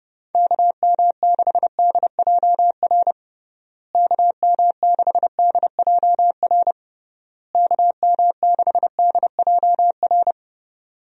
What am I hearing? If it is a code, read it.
Morse code, heard as a keyed tone, about 30 wpm, 700 Hz. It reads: KM6DJR KM6DJR KM6DJR
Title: USA Call Signs - 6 Characters - Repeated 3x - 30wpm